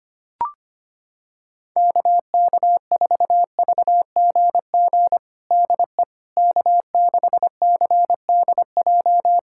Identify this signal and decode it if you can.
Morse code, heard as a keyed tone, about 25 wpm, 700 Hz. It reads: KK4VGG DE K6CDJ
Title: USA Callsign DE Callsign 25wpm